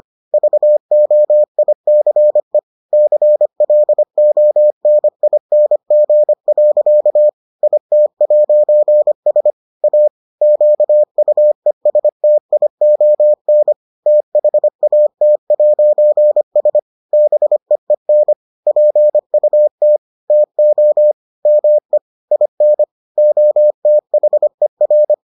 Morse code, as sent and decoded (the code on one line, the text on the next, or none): ...- --- .. -.-. / -.-. .-.. --- -. .. -. --. .-.-.- / .. - .----. ... / .- / --.- ..- . ... - .. --- -. / - .... .- - .----. ... / -... . -. / .--. ..- - / - --- / -- . / .. -. / --- - .... .-.
VOICE CLONING. IT'S A QUESTION THAT'S BEEN PUT TO ME IN OTHER